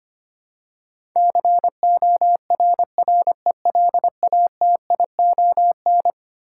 Morse code, heard as a keyed tone, about 25 wpm, 700 Hz. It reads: CORRELATION
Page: morse-code-ninja-practice-25wpm